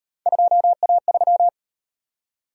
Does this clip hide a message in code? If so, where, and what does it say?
Yes: Morse code from 0.3 to 1.5 s, heard as a keyed tone, about 38 wpm, 700 Hz. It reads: 2A3